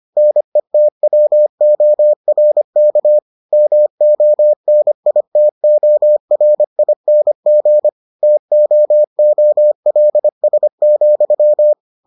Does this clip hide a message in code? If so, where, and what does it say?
Yes: Morse code from 0.2 to 11.7 s, heard as a keyed tone, about 25 wpm, 600 Hz. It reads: NETWORK MONITORING TOOLS,